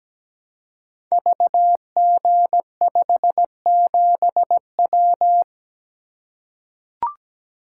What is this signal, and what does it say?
Morse code, heard as a keyed tone, about 17 wpm, 700 Hz. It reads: VG57W
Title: Code Groups - 5 Letter:Numbers - 17wpm